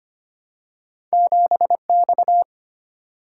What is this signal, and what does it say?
Morse code, heard as a keyed tone, about 25 wpm, 700 Hz. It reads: 7X
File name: Two Letter-Number Combinations - 25wpm